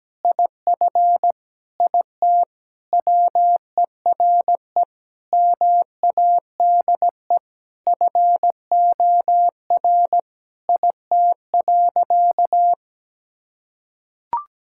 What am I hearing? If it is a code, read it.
Morse code, heard as a keyed tone, about 17 wpm, 700 Hz. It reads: IF IT WERE MADE FOR IT.
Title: Sentences from Top 1600 Words 17wpm